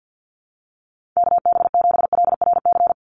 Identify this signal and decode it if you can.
Morse code, heard as a keyed tone, about 50 wpm, 700 Hz. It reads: KB7LRC
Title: USA Call Signs 50wpm